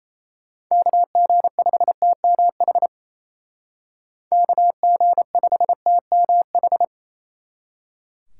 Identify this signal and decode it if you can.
Morse code, heard as a keyed tone, about 33 wpm, 700 Hz. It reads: KG5TMH KG5TMH
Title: Speed Racing - USA 2x3 Call Signs 28wpm